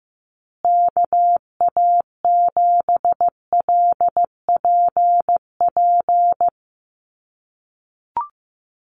Morse code, heard as a keyed tone, about 15 wpm, 700 Hz. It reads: KA7LPP
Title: USA Call Signs Repeated 2x 15wpm